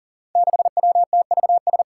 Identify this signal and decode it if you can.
Morse code, heard as a keyed tone, about 40 wpm, 700 Hz. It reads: 6WTVS
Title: Five Challenging Letter-Number Combinations 40wpm